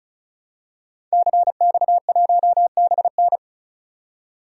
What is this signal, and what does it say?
Morse code, heard as a keyed tone, about 34 wpm, 700 Hz. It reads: CX1BN
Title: DXCC-100 Call Signs - 5 Characters - 35wpm